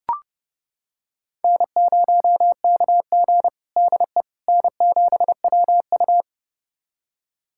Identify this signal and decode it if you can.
Morse code, heard as a keyed tone, about 30 wpm, 700 Hz. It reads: N0KG DE N7WU
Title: USA Callsign DE Callsign - 4 Characters - 30wpm